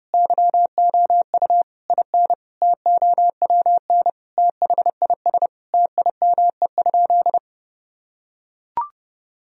Morse code, heard as a keyed tone, about 30 wpm, 700 Hz. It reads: YOU IN TOWN THIS TIME?
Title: Sentences from Top 1000 Words - Review Part 2 - 30wpm